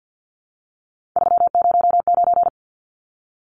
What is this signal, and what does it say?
Morse code, heard as a keyed tone, about 49 wpm, 700 Hz. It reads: 309
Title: Numbers - 3-Digit 50wpm